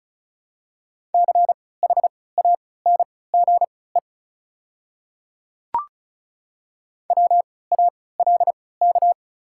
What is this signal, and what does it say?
Morse code, heard as a keyed tone, about 34 wpm, spaced as for 20 wpm, 700 Hz. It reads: CHANGE WALK
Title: 3x Character Spacing - Top 500 Words - 35wpm